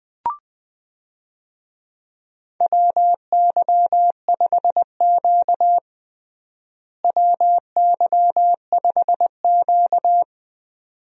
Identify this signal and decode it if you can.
Morse code, heard as a keyed tone, about 20 wpm, 700 Hz. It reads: WY5Q WY5Q
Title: USA Call Signs - 4 Characters - Repeated 2x - 20wpm